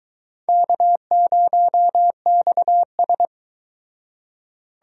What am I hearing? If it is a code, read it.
Morse code, heard as a keyed tone, about 22 wpm, 700 Hz. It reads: K0XS